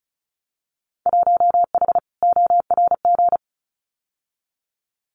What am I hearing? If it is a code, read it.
Morse code, heard as a keyed tone, about 34 wpm, 700 Hz. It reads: 1H ORG
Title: Winter Field Day Exchanges 35wpm